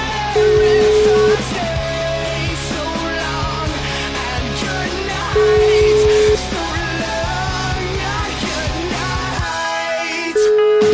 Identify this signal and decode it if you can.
Ringback tone (Europe, 425 Hz), with noise.